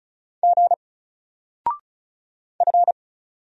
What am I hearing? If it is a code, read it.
Morse code, heard as a keyed tone, about 36 wpm, 700 Hz. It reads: G F
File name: Single-Letter-Number-35wpm